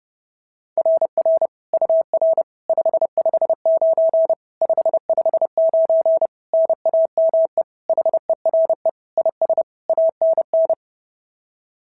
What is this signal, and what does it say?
Morse code, heard as a keyed tone, about 30 wpm, 650 Hz. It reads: RR UR 559 559 NAME HERE IS ANN
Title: SOTA QSO Practice 30wpm